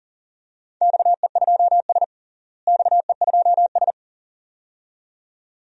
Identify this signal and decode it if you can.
Morse code, heard as a keyed tone, about 40 wpm, 700 Hz. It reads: XE2S XE2S
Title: DXCC-100 Call Signs - 4 Characters - Repeated 2x - 40wpm